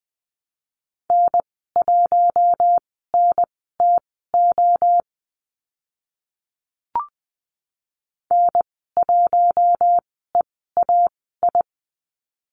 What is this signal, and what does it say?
Morse code, heard as a keyed tone, about 20 wpm, spaced as for 14 wpm, 700 Hz. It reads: N1NTO N1EAI
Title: 042 Lesson 8g - Number 1 - Call Signs 20wpm